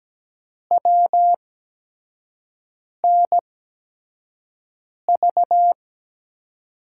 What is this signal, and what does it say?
Morse code, heard as a keyed tone, about 17 wpm, spaced as for 5 wpm, 700 Hz. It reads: WNV
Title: USA Call Signs - 8x Character Spacing - 17wpm